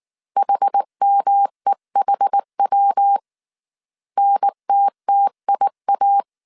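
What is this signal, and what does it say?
Morse code, heard as a keyed tone, about 19 wpm, 800 Hz. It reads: HMEHW NTTIA
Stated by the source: Morse code dictation at 18 wpm from the IW3SGT QRS Morse code training course, five-character groups of 300 characters.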